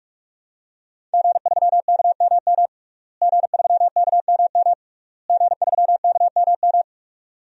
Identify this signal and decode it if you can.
Morse code, heard as a keyed tone, about 42 wpm, 700 Hz. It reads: G3KMM G3KMM G3KMM
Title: DXCC-100 Call Signs - 5 Characters - Repeated 3x - 45wpm